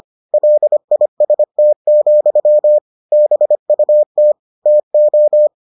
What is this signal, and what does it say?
Morse code, heard as a keyed tone, about 25 wpm, 600 Hz. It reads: LIST, BUT TO